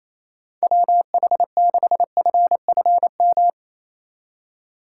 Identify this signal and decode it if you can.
Morse code, heard as a keyed tone, about 28 wpm, 700 Hz. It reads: WH6FFM
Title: USA Call Signs - 6 Characters - 3x Character Spacing - 28wpm